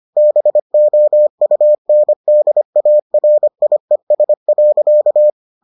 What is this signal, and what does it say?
Morse code, heard as a keyed tone, about 25 wpm, 600 Hz. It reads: BOUNDARIES.